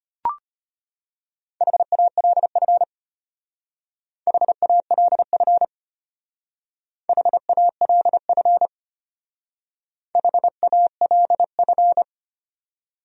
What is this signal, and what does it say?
Morse code, heard as a keyed tone, about 30 wpm, 700 Hz. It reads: HALF HALF HALF HALF
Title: Speed-Racing Top 500 Words 25wpm